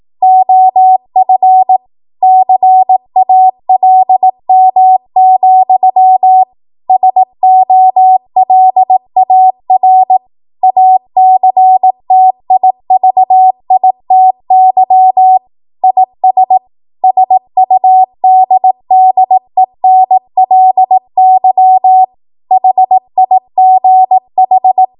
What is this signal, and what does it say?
Morse code, heard as a keyed tone, about 18 wpm, 750 Hz. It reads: OF CALM, SOLAR ACTIVITY IS SUDDENLY HIGH
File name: Morse / 18WPM